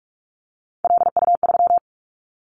Morse code, heard as a keyed tone, about 46 wpm, 690 Hz. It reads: LU3
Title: Callsign Prefixes for Top 50 DXCC - 45wpm